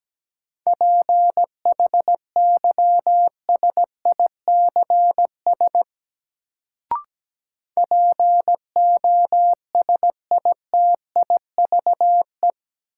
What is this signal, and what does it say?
Morse code, heard as a keyed tone, about 17 wpm, 700 Hz. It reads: PHYSICS POSITIVE
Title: Sentences from Top 1000 Words - Review Part 5 - New Word Drill - 17wpm